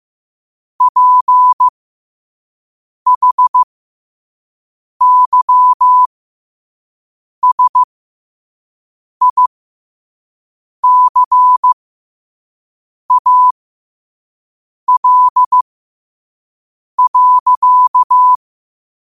Morse code, heard as a keyed tone, about 15 wpm, spaced as for 5 wpm, 1000 Hz. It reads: PHYSICAL.
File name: qotdfiles / 5wpm